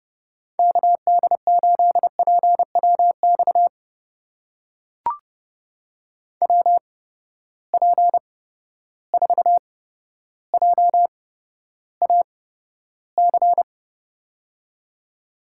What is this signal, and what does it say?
Morse code, heard as a keyed tone, about 30 wpm, spaced as for 13 wpm, 700 Hz. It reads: KD8PWX WP4JAC
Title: USA Call Signs - 6 Characters - 8x Character Spacing - 30wpm